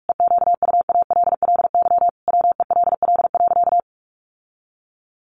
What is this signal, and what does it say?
Morse code, heard as a keyed tone, about 44 wpm, 700 Hz. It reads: EQUALLY WELL.
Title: Sentences from Top 1400 Words 45wpm